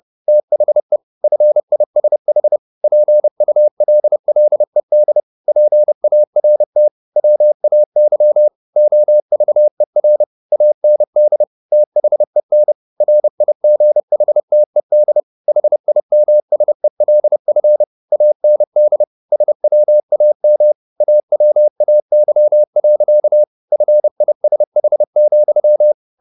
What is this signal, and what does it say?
Morse code, heard as a keyed tone, about 30 wpm, 600 Hz. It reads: THE FISH PULLED PART WAY OVER AND THEN RIGHTED HIMSELF AND SWAM AWAY. FISH,